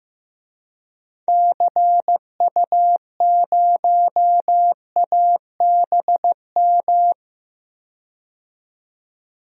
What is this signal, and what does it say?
Morse code, heard as a keyed tone, about 15 wpm, 700 Hz. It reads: CU0ABM
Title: DXCC-100 Call Signs - 6 Characters - 15wpm